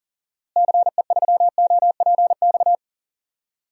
Morse code, heard as a keyed tone, about 40 wpm, 700 Hz. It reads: CE3OPX
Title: DXCC-100 Call Signs - 6 Characters - Repeated 2x - 40wpm